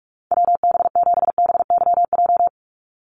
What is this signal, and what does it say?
Morse code, heard as a keyed tone, about 44 wpm, 700 Hz. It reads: WB7BYJ
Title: USA Call Signs 45wpm